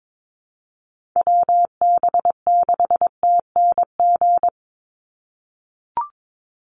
Morse code, heard as a keyed tone, about 22 wpm, 700 Hz. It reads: WB6TNG
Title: USA Call Signs 22wpm